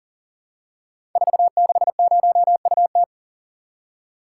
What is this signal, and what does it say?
Morse code, heard as a keyed tone, about 38 wpm, 690 Hz. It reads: 460UT
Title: Code Groups - 5 Letter:Numbers - 40wpm